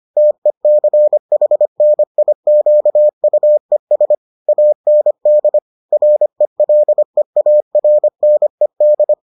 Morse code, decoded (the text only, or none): TECHNIQUES AND RELEARNED